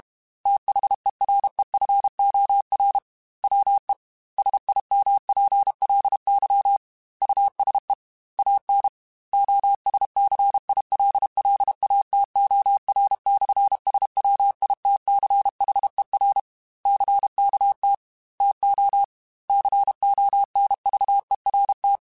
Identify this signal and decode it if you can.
Morse code, heard as a keyed tone, about 32 wpm, 800 Hz. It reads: THEREFOR WE SIMPLY USE AN OSCILLATOR/SWITCHER CKT TO CONVERT
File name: Technical Subjects / Technical Subjects-mp3s / Sw Pwr Sup @30